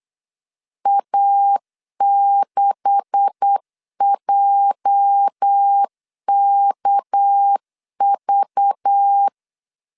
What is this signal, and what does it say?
Morse code, heard as a keyed tone, about 8 wpm, 800 Hz. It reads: A6JKV
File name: LEZIONI / 10 SIGNS / 10B INT 8wpm 3534